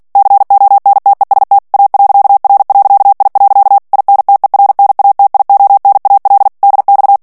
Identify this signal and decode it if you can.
Morse code, heard as a keyed tone, about 48 wpm, 800 Hz. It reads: CONTEST A.R.I. INTERNATIONAL DX
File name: ITALIEN texte clair 2 vitesse 45 mots minute